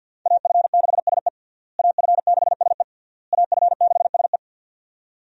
Morse code, heard as a keyed tone, about 53 wpm, 700 Hz. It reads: AF6SE AF6SE AF6SE
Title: USA Call Signs - 5 Characters - Repeated 3x - 50wpm